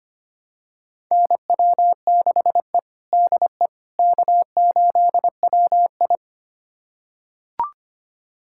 Morse code, heard as a keyed tone, about 25 wpm, 700 Hz. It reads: NW6E DE K8WI
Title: USA Callsign DE Callsign - 4 Characters - 25wpm